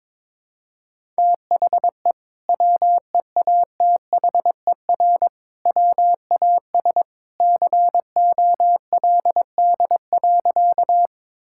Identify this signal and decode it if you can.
Morse code, heard as a keyed tone, about 22 wpm, 700 Hz. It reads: THE WEATHER WAS COLD.
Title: Sentences from Top 1000 Words - Review Part 5 - 22wpm